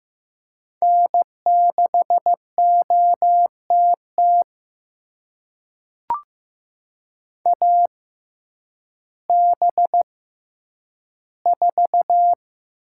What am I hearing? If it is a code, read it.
Morse code, heard as a keyed tone, about 15 wpm, spaced as for 9 wpm, 700 Hz. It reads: N6OTT AB4
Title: USA Call Signs - 5 Characters - 6x Character Spacing - 15wpm